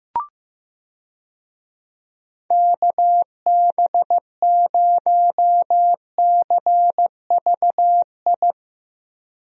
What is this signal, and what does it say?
Morse code, heard as a keyed tone, about 16 wpm, 700 Hz. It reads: KB0CVI
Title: USA Call Signs - 6 Characters - Repeated 2x - 15wpm